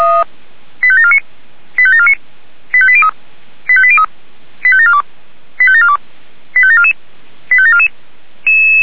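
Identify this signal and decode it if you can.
Five-tone selective call (ZVEI), heard as quick runs of five tones, tones of about 70 ms. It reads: 85749 85749 85803 85803 85742 85742 85744 85744
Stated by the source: ZVEI five-tone selective call received on an SDR.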